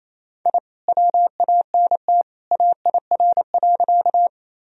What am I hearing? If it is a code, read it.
Morse code, heard as a keyed tone, about 28 wpm, 700 Hz. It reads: I WANT AIR.
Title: Sentences from Top 1000 Words - Review Part 1 - 28wpm